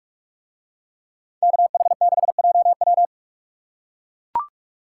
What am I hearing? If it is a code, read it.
Morse code, heard as a keyed tone, about 41 wpm, 700 Hz. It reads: KH6JW